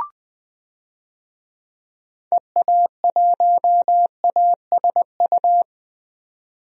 Morse code, heard as a keyed tone, about 20 wpm, 700 Hz. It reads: EA1ASU
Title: DXCC-100 Call Signs - 6 Characters - 20wpm